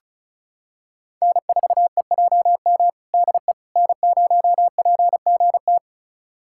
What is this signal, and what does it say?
Morse code, heard as a keyed tone, about 36 wpm, 700 Hz. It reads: N4EJM DE N0PGT
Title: USA Callsign DE Callsign - 5 Characters - 35wpm